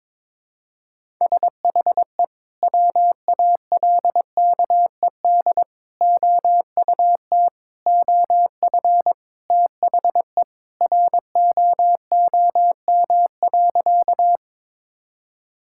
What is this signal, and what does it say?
Morse code, heard as a keyed tone, about 23 wpm, 700 Hz. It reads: SHE WALKED OUT OF THE ROOM.
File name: practice / 22wpm / Sentences with 3-Word Phrases - 22wpm